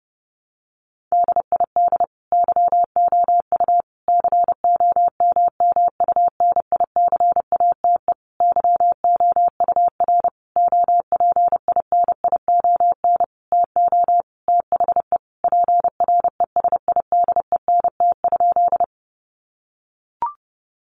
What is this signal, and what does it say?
Morse code, heard as a keyed tone, about 28 wpm, 700 Hz. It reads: DID YOU COMMUNICATE YOUR OPINION TO THE PRESIDENT?
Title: Sentences from Top 1300 Words 30wpm